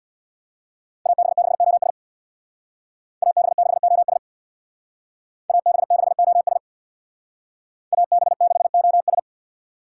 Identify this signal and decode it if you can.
Morse code, heard as a keyed tone, about 64 wpm, 700 Hz. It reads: AB6OS AB6OS AB6OS AB6OS